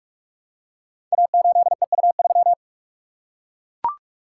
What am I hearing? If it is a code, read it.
Morse code, heard as a keyed tone, about 44 wpm, 700 Hz. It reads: A8EU3
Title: Code Groups - 5 Letter:Numbers - 45wpm